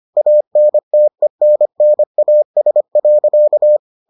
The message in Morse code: ANTENNAS.

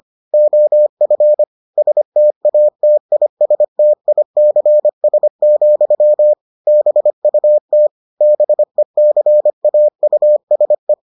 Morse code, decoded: OF STATISTICS, BUT BECAUSE

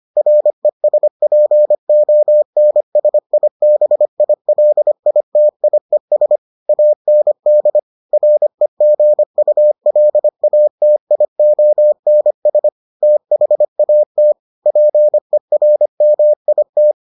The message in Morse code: RESPONSIBILITIES AND REGULATIONS THAT PERMIT

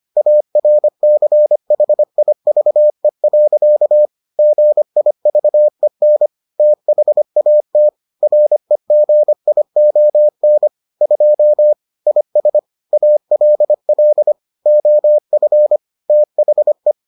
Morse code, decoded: ARCHIVE. GIVEN THAT REGION 2 IS ALL OF THE

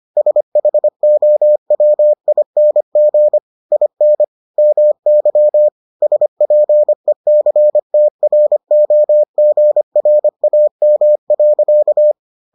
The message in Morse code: SHOWING IN MY SPECTROGRAM.

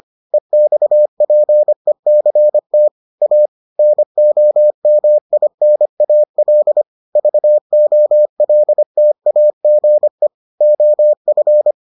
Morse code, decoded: EXPECT A NOMINAL VOLTAGE OF

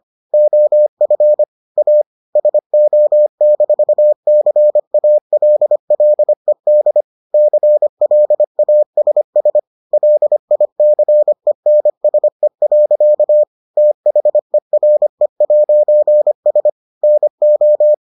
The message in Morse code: OF A SO-CALLED CLASS LICENSE. THERE'S NO

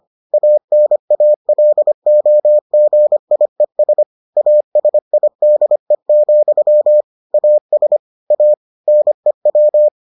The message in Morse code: ANALOGIES ASIDE, AS A NEW

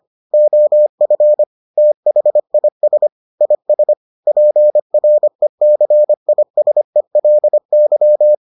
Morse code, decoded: OF THIS IS PRECISELY

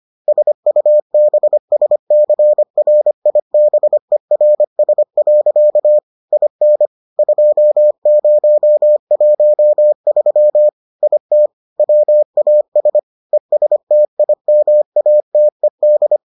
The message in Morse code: SUBSCRIBERS. IN 2013 IT WAS ESTIMATED

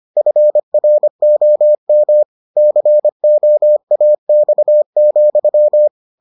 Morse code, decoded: FROM COAX,

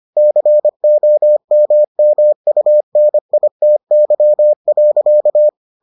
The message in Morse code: COMMUNITY.

